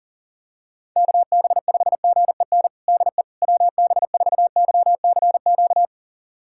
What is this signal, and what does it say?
Morse code, heard as a keyed tone, about 40 wpm, 700 Hz. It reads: KB5GEN DE WB4YCQ
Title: USA Callsign DE Callsign - 6 Characters - 40wpm